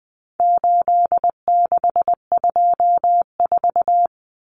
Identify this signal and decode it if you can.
Morse code, heard as a keyed tone, about 20 wpm, 700 Hz. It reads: 8624
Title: Common Cut Numbers - 4-Digit 20wpm